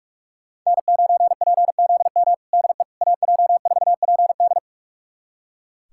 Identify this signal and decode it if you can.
Morse code, heard as a keyed tone, about 41 wpm, 700 Hz. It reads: N9PZM DE AJ4PD
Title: USA Callsign DE Callsign - 5 Characters - 45wpm